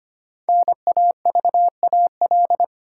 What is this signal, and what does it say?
Morse code, heard as a keyed tone, about 25 wpm, 700 Hz. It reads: NAVAL
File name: 5 Letter Words - First 3 Letters Given - 25wpm